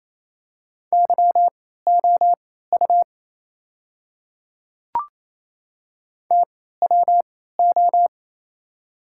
Morse code, heard as a keyed tone, about 28 wpm, spaced as for 16 wpm, 700 Hz. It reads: YOU TWO
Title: 3x Character Spacing - Top 100 Words - 28wpm